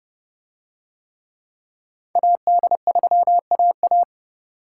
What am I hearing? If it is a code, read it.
Morse code, heard as a keyed tone, about 30 wpm, 700 Hz. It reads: AD3AA